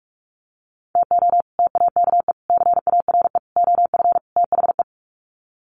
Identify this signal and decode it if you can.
Morse code, heard as a keyed tone, about 44 wpm, 700 Hz. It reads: TO TAKE CARE OF THE
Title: Sets of 5 Words Encore 45wpm